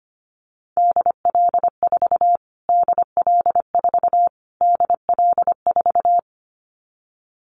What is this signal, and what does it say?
Morse code, heard as a keyed tone, about 25 wpm, 700 Hz. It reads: DL4 DL4 DL4
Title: Callsign Prefixes for Top 25 DXCC - Repeated 3x - 25wpm